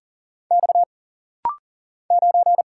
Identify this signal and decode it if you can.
Morse code, heard as a keyed tone, about 40 wpm, 700 Hz. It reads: X 9
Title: Single Letter-Number Alt - Rapid-Fire 40wpm